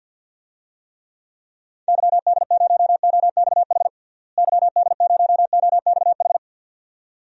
Morse code, heard as a keyed tone, about 52 wpm, 700 Hz. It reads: YD0OXH YD0OXH
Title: DXCC-100 Call Signs - 6 Characters - Repeated 2x - 50wpm